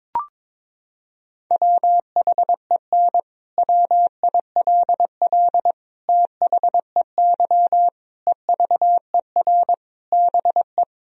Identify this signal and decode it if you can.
Morse code, heard as a keyed tone, about 22 wpm, 700 Hz. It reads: WHEN WILL THEY EVER BE
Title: Sentences from Top 1000 Words - Review Part 4 - 22wpm